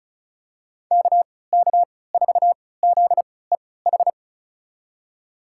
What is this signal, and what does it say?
Morse code, heard as a keyed tone, about 36 wpm, spaced as for 20 wpm, 700 Hz. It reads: KK4ZEH
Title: USA Call Signs - 6 Characters - 3x Character Spacing - 35wpm